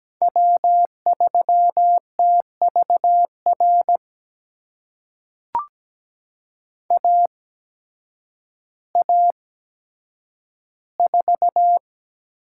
Morse code, heard as a keyed tone, about 17 wpm, spaced as for 9 wpm, 700 Hz. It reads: W3TVR AA4
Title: USA Call Signs - 5 Characters - 8x Character Spacing - 17wpm